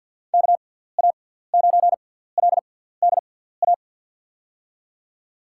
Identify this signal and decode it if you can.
Morse code, heard as a keyed tone, about 49 wpm, spaced as for 17 wpm, 700 Hz. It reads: KA9LDA
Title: USA Call Signs - 6 Characters - 6x Character Spacing - 50wpm